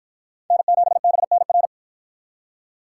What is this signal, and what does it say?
Morse code, heard as a keyed tone, about 51 wpm, 700 Hz. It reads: N7BNR